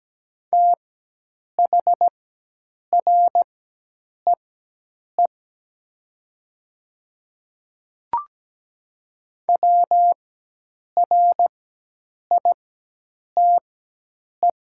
Morse code, heard as a keyed tone, about 17 wpm, spaced as for 8 wpm, 700 Hz. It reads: THREE WRITE
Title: 4x Character Spacing - Top 200 Words - 17wpm